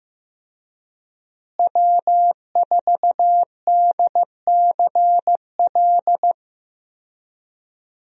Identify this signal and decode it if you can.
Morse code, heard as a keyed tone, about 16 wpm, 700 Hz. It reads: W4DCL
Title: USA Call Signs - 5 Characters - Repeated 2x - 15wpm